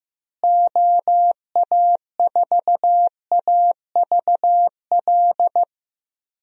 Morse code, heard as a keyed tone, about 15 wpm, 700 Hz. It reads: OA4AVL
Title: USA Call Signs - 6 Characters - Repeated 3x - 15wpm